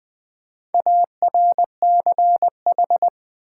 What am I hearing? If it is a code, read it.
Morse code, heard as a keyed tone, about 20 wpm, 700 Hz. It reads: ARCH
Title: Build A Word - 4 Letters At A Time - 12-Letter Words - 20wpm